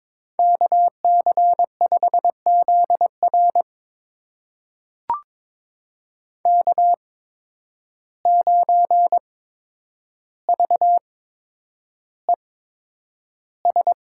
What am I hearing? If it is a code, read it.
Morse code, heard as a keyed tone, about 21 wpm, spaced as for 9 wpm, 700 Hz. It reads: KC5ZR K9VES
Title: USA Call Signs - 5 Characters - 8x Character Spacing - 22wpm